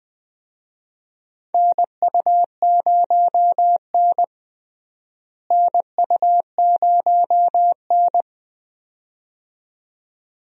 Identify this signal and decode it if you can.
Morse code, heard as a keyed tone, about 20 wpm, 700 Hz. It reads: NU0N NU0N
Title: USA Call Signs - 4 Characters - Repeated 2x - 20wpm